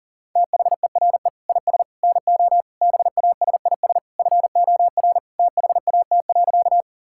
Morse code, heard as a keyed tone, about 40 wpm, 700 Hz. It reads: THERE IS NO BASIS FOR THAT.
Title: Sentences from Top 1000 Words - Review Part 5 - 40wpm